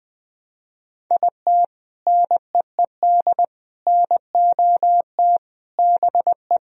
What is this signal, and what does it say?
Morse code, heard as a keyed tone, about 21 wpm, 700 Hz. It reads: IT NEED NOT BE
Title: Sentences from Top 1000 Words - Review Part 4 - 20wpm